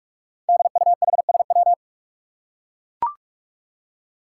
Morse code, heard as a keyed tone, about 44 wpm, 700 Hz. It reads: DUHSW
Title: Five Challenging Letter-Number Combinations 45wpm